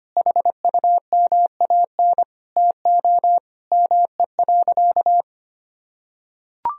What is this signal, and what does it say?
Morse code, heard as a keyed tone, about 25 wpm, 700 Hz. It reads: HUMAN TO ME.
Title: Sentences from Top 1000 Words - Review Part 3 - 25wpm